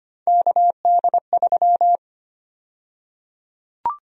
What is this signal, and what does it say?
Morse code, heard as a keyed tone, about 25 wpm, 700 Hz. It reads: KD3
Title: USA Call Sign Prefixes - 25wpm